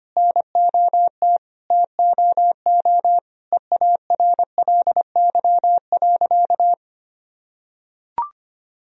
Morse code, heard as a keyed tone, about 25 wpm, 700 Hz. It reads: NOT TOO EARLY.
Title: Sentences from Top 1000 Words - Review Part 2 - 25wpm